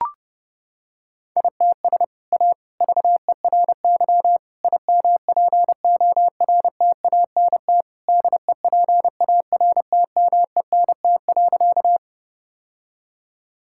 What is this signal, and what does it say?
Morse code, heard as a keyed tone, about 30 wpm, 700 Hz. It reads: ITS A VERY IMPORTANT DEPARTMENT.
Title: Sentences from Top 1000 Words - Review Part 5 - 30wpm